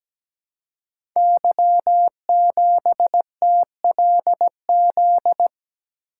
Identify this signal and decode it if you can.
Morse code, heard as a keyed tone, about 17 wpm, 700 Hz. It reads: Y7TLZ
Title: Code Groups - 5 Letter:Numbers - 17wpm